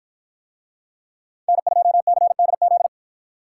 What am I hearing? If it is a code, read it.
Morse code, heard as a keyed tone, about 49 wpm, 700 Hz. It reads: N2CDZ